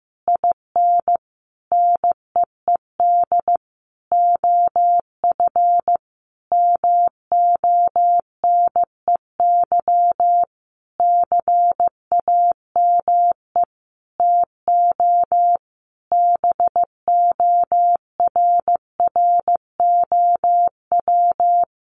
Morse code, heard as a keyed tone, about 15 wpm, 700 Hz. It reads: IN NEED OF MONEY CAME TO BORROW
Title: Sentences from Top 1500 Words 15wpm